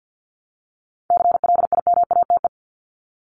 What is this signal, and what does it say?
Morse code, heard as a keyed tone, about 50 wpm, 700 Hz. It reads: CLIMATE
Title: Top 1501-1750 Words 50wpm